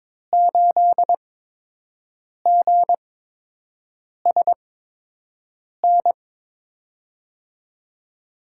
Morse code, heard as a keyed tone, about 21 wpm, spaced as for 6 wpm, 700 Hz. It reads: 8GSN
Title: USA Call Signs - 8x Character Spacing - 22wpm